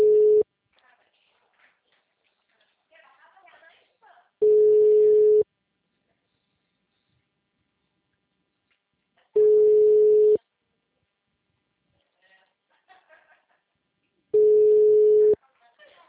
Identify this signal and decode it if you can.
Ringback tone (Europe, 425 Hz), with speech.